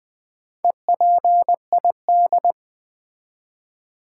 Morse code, heard as a keyed tone, about 20 wpm, 700 Hz. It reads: EPID
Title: Build A Word - 2 Letters At A Time - 8-Letter Words - 20wpm